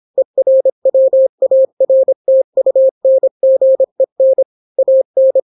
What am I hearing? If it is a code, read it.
Morse code, heard as a keyed tone, about 26 wpm, 530 Hz. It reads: ERWARTUNGEN AN